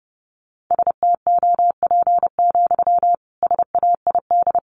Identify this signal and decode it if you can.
Morse code, heard as a keyed tone, about 30 wpm, 700 Hz. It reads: STOP, SAID
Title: Sentences from Top 300 Words 30wpm